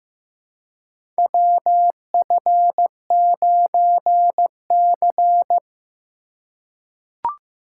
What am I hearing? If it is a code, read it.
Morse code, heard as a keyed tone, about 15 wpm, 700 Hz. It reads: WF9C
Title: USA Call Signs - 4 Characters - 8x Character Spacing - 15wpm